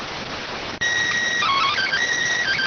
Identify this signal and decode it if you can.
SSTV (slow-scan television), PD 290.